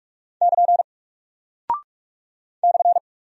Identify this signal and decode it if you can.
Morse code, heard as a keyed tone, about 44 wpm, 700 Hz. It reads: <KN> /
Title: Single Character (Letter:Number:Punctuation:Prosign) Alt 45wpm